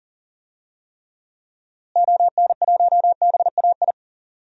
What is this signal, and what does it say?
Morse code, heard as a keyed tone, about 40 wpm, 700 Hz. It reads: ON1BAI
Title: DXCC-100 Call Signs - 6 Characters - Repeated 2x - 40wpm